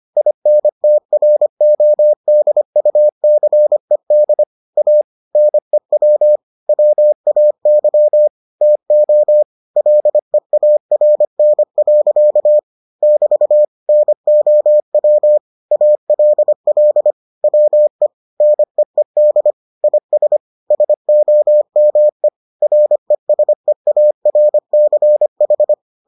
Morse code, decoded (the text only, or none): INTRODUCED A NEW WAY TO LEARN. <BT> NOW ALL WE NEED IS SOME RESEARCH